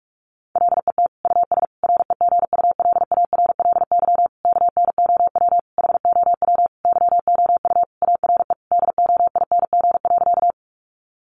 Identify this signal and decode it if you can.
Morse code, heard as a keyed tone, about 44 wpm, 700 Hz. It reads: LET US REGULARLY KNOW HOW YOU ARE DOING.
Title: Sentences from Top 1400 Words 45wpm